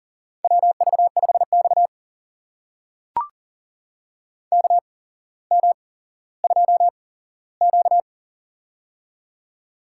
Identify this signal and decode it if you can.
Morse code, heard as a keyed tone, about 41 wpm, spaced as for 17 wpm, 700 Hz. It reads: WV5X KM2Q
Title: USA Call Signs - 4 Characters - 8x Character Spacing - 40wpm